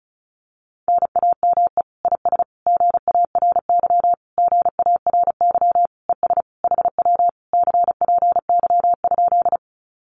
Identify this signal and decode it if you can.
Morse code, heard as a keyed tone, about 34 wpm, 700 Hz. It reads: NAME IS GARY GARY ES HW CPY?
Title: Common QSO Phrases Encore 35wpm